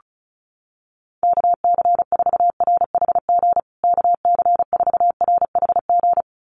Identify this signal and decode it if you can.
Morse code, heard as a keyed tone, about 34 wpm, 700 Hz. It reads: KC4RHG KC4RHG
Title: USA Call Signs Repeated 2x 35wpm